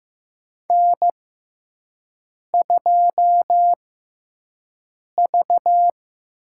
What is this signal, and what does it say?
Morse code, heard as a keyed tone, about 15 wpm, spaced as for 5 wpm, 700 Hz. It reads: N2V
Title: USA Call Signs - 4 Characters - 6x Character Spacing - 15wpm